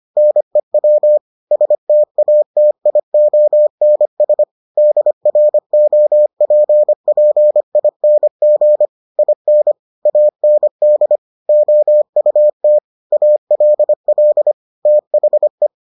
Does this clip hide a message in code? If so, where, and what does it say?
Yes: Morse code from 0.2 to 15.7 s, heard as a keyed tone, about 25 wpm, 600 Hz. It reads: NEW STATIONS DROPPING IN AND OUT ALL THE